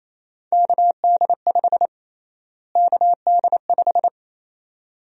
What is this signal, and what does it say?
Morse code, heard as a keyed tone, about 27 wpm, 700 Hz. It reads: KD5 KD5
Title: USA Call Sign Prefixes - Repeated 2x - 28wpm